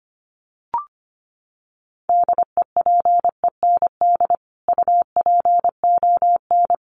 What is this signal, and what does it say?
Morse code, heard as a keyed tone, about 25 wpm, 700 Hz. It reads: DEPEND UPON
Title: Sentences from Top 1300 Words 25wpm